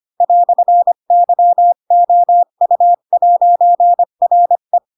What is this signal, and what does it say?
Morse code, heard as a keyed tone, about 25 wpm, 700 Hz. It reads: "YOU'RE